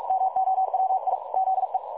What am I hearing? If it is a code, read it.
Morse code, heard as a keyed tone, about 46 wpm, 760 Hz. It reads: MOWI ZE
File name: dlazaawansowanych / nagraniazpasma / websdr recording 3565.0kHz